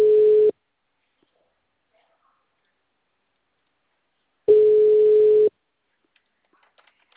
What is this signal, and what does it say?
Ringback tone (Europe, 425 Hz).
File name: out- -2037- - - .1032